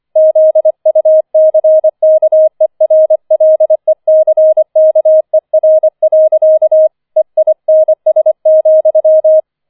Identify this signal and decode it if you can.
Morse code, heard as a keyed tone, about 25 wpm, 630 Hz. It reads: ZUCKERLECKER. EINS,
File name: 25wpm